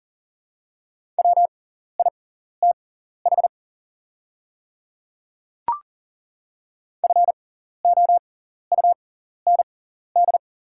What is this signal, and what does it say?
Morse code, heard as a keyed tone, about 40 wpm, spaced as for 14 wpm, 700 Hz. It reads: WITH FOUND
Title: 6x Character Spacing - Top 300 Words - 40wpm